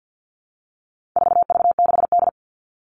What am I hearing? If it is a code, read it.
Morse code, heard as a keyed tone, about 50 wpm, 700 Hz. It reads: 4V6D
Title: All Cut Numbers - 4-Digit 50wpm